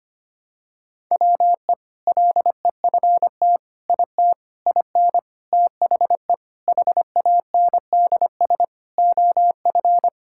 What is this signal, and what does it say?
Morse code, heard as a keyed tone, about 25 wpm, 700 Hz. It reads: WE LEFT IT IN THE HANDS OF
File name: practice / 25wpm / Sentences with 4-Word Phrases - 25wpm